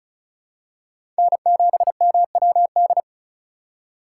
Morse code, heard as a keyed tone, about 34 wpm, 700 Hz. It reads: N7MWD
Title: USA Call Signs - 5 Characters - 4x Character Spacing - 35wpm